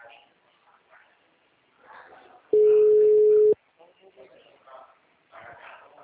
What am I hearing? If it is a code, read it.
Ringback tone (Europe, 425 Hz), with speech.